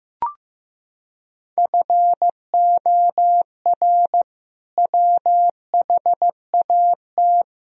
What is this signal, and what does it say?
Morse code, heard as a keyed tone, about 15 wpm, 700 Hz. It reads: FOR WHAT
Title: Sentences from Top 1000 Words - Review Part 4 - 15wpm